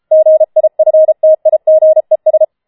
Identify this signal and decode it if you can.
Morse code, heard as a keyed tone, about 33 wpm, 630 Hz. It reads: GIFTIGES
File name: Schneewittchen Teil5 34wpm